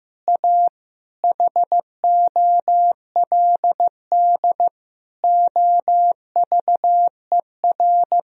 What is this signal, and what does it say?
Morse code, heard as a keyed tone, about 15 wpm, 700 Hz. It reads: A HOLD OVER
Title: Sentences from Top 1000 Words - Review Part 2 - 15wpm